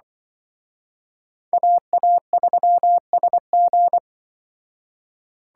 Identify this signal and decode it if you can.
Morse code, heard as a keyed tone, about 24 wpm, 700 Hz. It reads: AA3SG
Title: Speed Racing - USA 2x2 Call Signs 20wpm